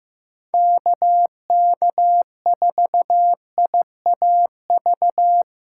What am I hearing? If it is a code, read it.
Morse code, heard as a keyed tone, about 15 wpm, 700 Hz. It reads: KK4IAV
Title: USA Call Signs - 6 Characters - 4x Character Spacing - 15wpm